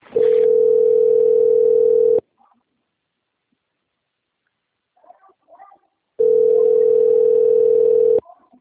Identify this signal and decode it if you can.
Ringback tone (North America), 440 + 480 Hz.